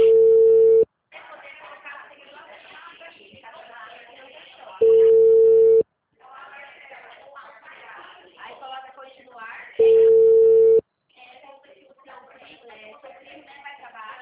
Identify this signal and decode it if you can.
Ringback tone (China, 450 Hz), with speech.